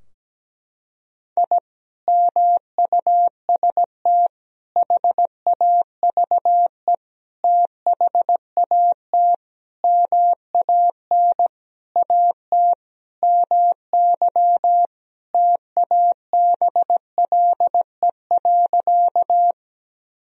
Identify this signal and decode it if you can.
Morse code, heard as a keyed tone, about 17 wpm, 700 Hz. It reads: I MUST HAVE THAT MAN AT MY TABLE.
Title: Sentences from Top 1000 Words - Review Part 2 - 17wpm